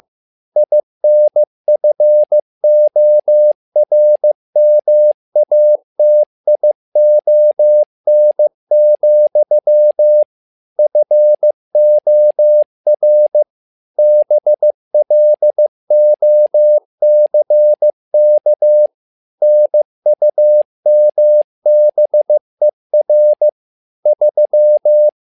Morse code, heard as a keyed tone, about 15 wpm, 600 Hz. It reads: INFORMATION, FOR BLOCK NUMBER 3
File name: learning center / tower trouble / tower trouble 15wpm